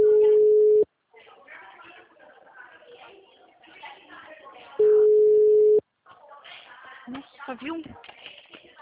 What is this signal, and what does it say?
Ringback tone (Europe, 425 Hz), with speech.
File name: out- -2008- - - .395